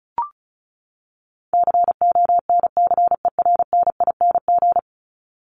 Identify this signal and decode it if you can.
Morse code, heard as a keyed tone, about 34 wpm, 700 Hz. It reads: CONCERNING
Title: ING Suffix - Top 100 Words with Base Words 35wpm